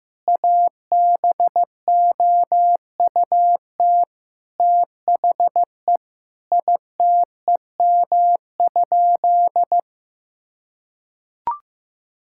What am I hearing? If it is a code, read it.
Morse code, heard as a keyed tone, about 15 wpm, 700 Hz. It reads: ABOUT THE ITEM?
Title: Sentences from Top 1000 Words - Review Part 6 - 15wpm